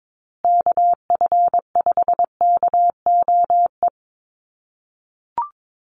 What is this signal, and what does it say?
Morse code, heard as a keyed tone, about 22 wpm, 700 Hz. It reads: KF5KOE
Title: USA Call Signs Repeated 3x 22wpm